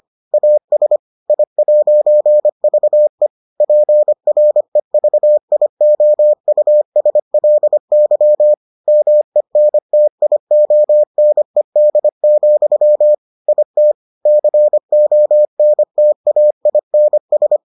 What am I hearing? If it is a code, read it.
Morse code, heard as a keyed tone, about 25 wpm, 600 Hz. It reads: AS I'VE PREVIOUSLY MENTIONED, IT CONTAINS